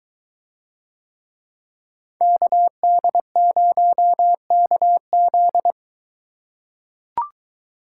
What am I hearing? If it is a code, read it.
Morse code, heard as a keyed tone, about 23 wpm, 700 Hz. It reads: KD0KZ